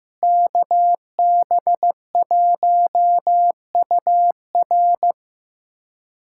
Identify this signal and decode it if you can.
Morse code, heard as a keyed tone, about 15 wpm, 700 Hz. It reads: KB1UR
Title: USA Callsign DE Callsign - 5 Characters - 15wpm